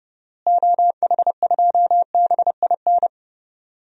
Morse code, heard as a keyed tone, about 30 wpm, 700 Hz. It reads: OH2BIN
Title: DXCC-100 Call Signs - 6 Characters - Repeated 3x - 30wpm